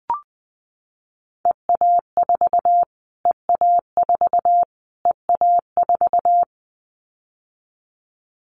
Morse code, heard as a keyed tone, about 20 wpm, 700 Hz. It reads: EA4 EA4 EA4
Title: Callsign Prefixes for Top 25 DXCC - Repeated 3x - 20wpm